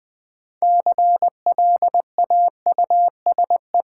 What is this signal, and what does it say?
Morse code, heard as a keyed tone, about 20 wpm, 700 Hz. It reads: CLAUSE
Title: Build A Word - 2 Letters At A Time - 6-Letter Words - 20wpm